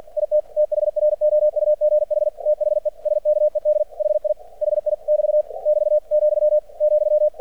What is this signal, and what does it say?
Morse code, heard as a keyed tone, about 50 wpm, 600 Hz. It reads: IT THROUGH THE SKED HI HI <BT> <BT> , ,
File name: archived / CRSnet